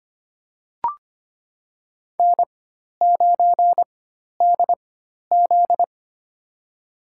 Morse code, heard as a keyed tone, about 25 wpm, spaced as for 12 wpm, 700 Hz. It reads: N9DZ
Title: USA Call Signs - 4 Characters - 4x Character Spacing - 25wpm